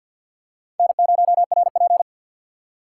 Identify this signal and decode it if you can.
Morse code, heard as a keyed tone, about 53 wpm, 690 Hz. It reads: N0RP
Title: USA Call Signs - 4 Characters - 6x Character Spacing - 50wpm